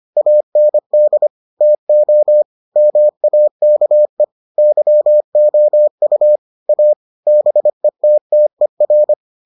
Morse code, heard as a keyed tone, about 25 wpm, 600 Hz. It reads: AND TO MAKE YOU A BETTER